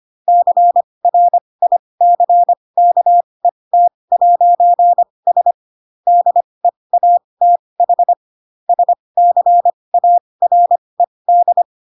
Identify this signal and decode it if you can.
Morse code, heard as a keyed tone, about 25 wpm, spaced as for 20 wpm, 700 Hz. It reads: CRICKET'S DEATH SCARED
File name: Pinocchio / Pinocchio ch5 15 25 ik1hge